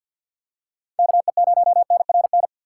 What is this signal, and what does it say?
Morse code, heard as a keyed tone, about 47 wpm, 700 Hz. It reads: KE0NRN
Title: USA Call Signs - 6 Characters - 4x Character Spacing - 50wpm